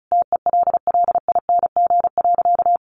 Morse code, heard as a keyed tone, about 34 wpm, 700 Hz. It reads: TELLING.
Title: A Sound Of Thunder 35wpm